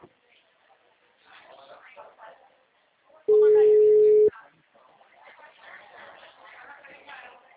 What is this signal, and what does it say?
Ringback tone (Europe, 425 Hz), with speech.